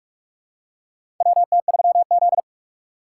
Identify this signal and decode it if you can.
Morse code, heard as a keyed tone, about 46 wpm, 700 Hz. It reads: WT3Z
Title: USA Call Signs - 4 Characters - 2x Character Spacing - 45wpm